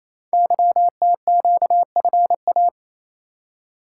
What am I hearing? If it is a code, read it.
Morse code, heard as a keyed tone, about 27 wpm, 700 Hz. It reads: YTQFA
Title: Code Groups - 5 Letters - 28wpm